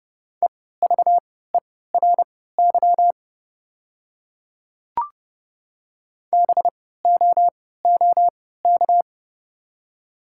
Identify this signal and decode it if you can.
Morse code, heard as a keyed tone, about 30 wpm, spaced as for 17 wpm, 700 Hz. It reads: EVERY BOOK
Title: 3x Character Spacing - Top 300 Words - 30wpm